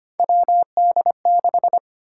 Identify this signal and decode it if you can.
Morse code, heard as a keyed tone, about 25 wpm, 700 Hz. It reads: WD6
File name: Three Letter-Number Combinations - 25wpm